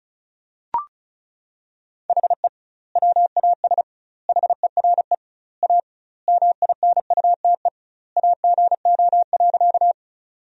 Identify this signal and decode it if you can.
Morse code, heard as a keyed tone, about 36 wpm, 700 Hz. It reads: HE WAS HERE A MINUTE AGO.
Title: 2x Word-Spacing - Sentences from Top 500 Words - 35wpm